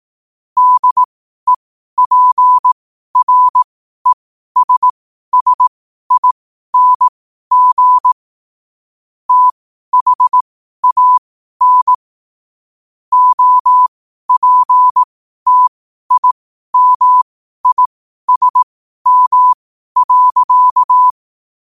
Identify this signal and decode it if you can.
Morse code, heard as a keyed tone, about 18 wpm, spaced as for 12 wpm, 1000 Hz. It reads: DEPRESSING THAN OPTIMISM.
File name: qotdfiles / 12wpm